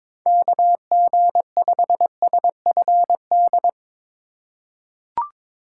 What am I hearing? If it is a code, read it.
Morse code, heard as a keyed tone, about 22 wpm, 700 Hz. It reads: KG5SFD
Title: USA Call Signs - 6 Characters - 4x Character Spacing - 22wpm